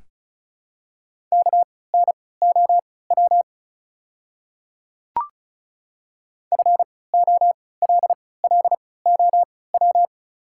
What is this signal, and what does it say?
Morse code, heard as a keyed tone, about 36 wpm, spaced as for 20 wpm, 700 Hz. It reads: KNOW FOLLOW
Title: 3x Character Spacing - Top 300 Words - 35wpm